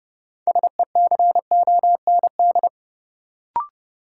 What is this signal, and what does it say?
Morse code, heard as a keyed tone, about 31 wpm, 700 Hz. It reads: SECOND
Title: Sentences from Top 300 Words - New Word Drill - 30wpm